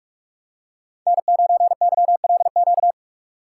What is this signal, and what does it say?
Morse code, heard as a keyed tone, about 46 wpm, 700 Hz. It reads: N9YLQ
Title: USA Call Signs - 3x Character Spacing - 45wpm